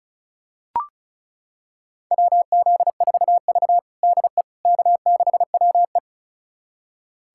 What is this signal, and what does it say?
Morse code, heard as a keyed tone, about 36 wpm, 700 Hz. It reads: WZ4V DE K6WE
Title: USA Callsign DE Callsign - 4 Characters - 35wpm